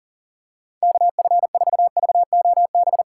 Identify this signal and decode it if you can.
Morse code, heard as a keyed tone, about 40 wpm, 700 Hz. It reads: KF4VOB
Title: USA Call Signs - 6 Characters - 3x Character Spacing - 40wpm